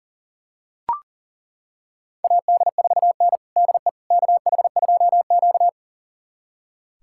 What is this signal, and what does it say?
Morse code, heard as a keyed tone, about 37 wpm, 700 Hz. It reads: AD4N DE KH2Q
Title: USA Callsign DE Callsign - 4 Characters - 40wpm